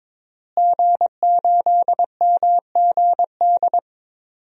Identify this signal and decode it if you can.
Morse code, heard as a keyed tone, about 22 wpm, 700 Hz. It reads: G8MGD
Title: DXCC-100 Call Signs - 5 Characters - Repeated 2x - 22wpm